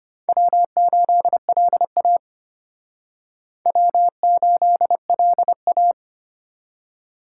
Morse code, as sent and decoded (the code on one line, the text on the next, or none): .-- ---.. .-.. .- / .-- ---.. .-.. .-
W8LA W8LA